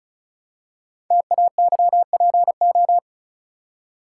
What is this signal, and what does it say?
Morse code, heard as a keyed tone, about 36 wpm, 690 Hz. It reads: TAYPO
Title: Code Groups - 5 Letters - 35wpm